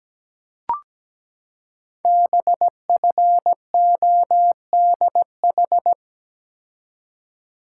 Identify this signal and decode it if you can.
Morse code, heard as a keyed tone, about 17 wpm, 700 Hz. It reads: BFODH
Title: Code Groups - 5 Letters - 17wpm